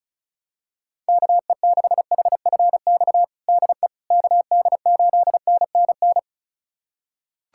Morse code, heard as a keyed tone, about 36 wpm, 700 Hz. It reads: KE6HFX DE KD8NNN
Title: USA Callsign DE Callsign - 6 Characters - 35wpm